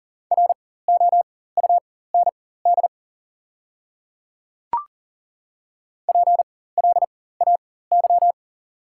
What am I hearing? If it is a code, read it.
Morse code, heard as a keyed tone, about 40 wpm, spaced as for 19 wpm, 690 Hz. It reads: ROUND PLAY
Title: 4x Character Spacing - Top 300 Words - 40wpm